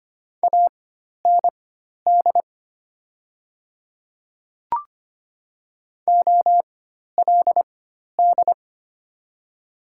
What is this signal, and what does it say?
Morse code, heard as a keyed tone, about 25 wpm, spaced as for 12 wpm, 700 Hz. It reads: AND OLD
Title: 4x Character Spacing - Top 200 Words - 25wpm